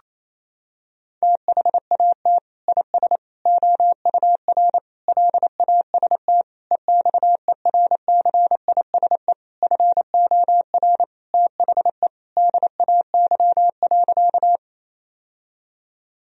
Morse code, heard as a keyed tone, about 28 wpm, 700 Hz. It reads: THAT IS OUR LAST EXERCISE FOR THE DAY.
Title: Sentences from Top 1000 Words - Review Part 4 - 28wpm